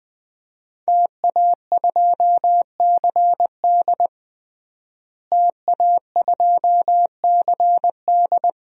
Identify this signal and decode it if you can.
Morse code, heard as a keyed tone, about 21 wpm, 700 Hz. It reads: TA2CD TA2CD
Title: DXCC-100 Call Signs - 5 Characters - Repeated 2x - 20wpm